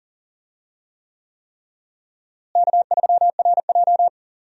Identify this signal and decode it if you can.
Morse code, heard as a keyed tone, about 40 wpm, 700 Hz. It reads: K3RJ
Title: USA Call Signs - 4 Characters - Repeated 3x - 40wpm